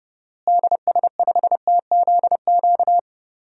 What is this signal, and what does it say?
Morse code, heard as a keyed tone, about 30 wpm, 700 Hz. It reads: DS5TZQ
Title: DXCC-100 Call Signs - 6 Characters - Repeated 3x - 30wpm